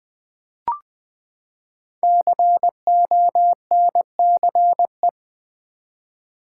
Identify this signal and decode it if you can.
Morse code, heard as a keyed tone, about 20 wpm, 700 Hz. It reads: CONCE
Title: Build A Word - 5 Letters At A Time - 10-Letter Words - 20wpm